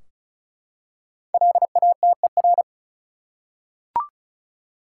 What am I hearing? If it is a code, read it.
Morse code, heard as a keyed tone, about 36 wpm, 700 Hz. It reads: LATER
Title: Five Letter Words - 35wpm